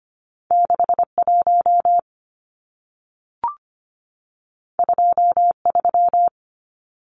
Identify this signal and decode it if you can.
Morse code, heard as a keyed tone, about 25 wpm, 700 Hz. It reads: 61 23
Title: Numbers - 2-Digit 25wpm